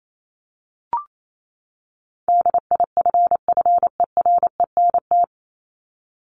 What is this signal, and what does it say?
Morse code, heard as a keyed tone, about 28 wpm, 700 Hz. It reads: DIFFERENT
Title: Top 501-750 Words 28wpm